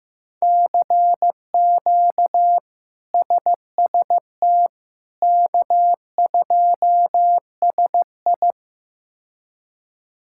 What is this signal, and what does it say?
Morse code, heard as a keyed tone, about 15 wpm, 700 Hz. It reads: CQ SST K2SI
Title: K1USN SST QSO Practice - 15wpm